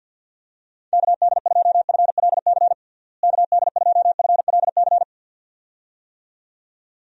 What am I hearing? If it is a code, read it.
Morse code, heard as a keyed tone, about 51 wpm, 700 Hz. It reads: KD2FLC KD2FLC
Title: USA Call Signs - 6 Characters - Repeated 2x - 50wpm